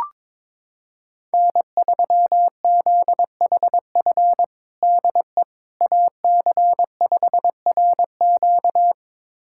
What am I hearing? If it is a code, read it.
Morse code, heard as a keyed tone, about 22 wpm, 700 Hz. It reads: N3ZHF DE AC5RQ
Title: USA Callsign DE Callsign - 5 Characters - 22wpm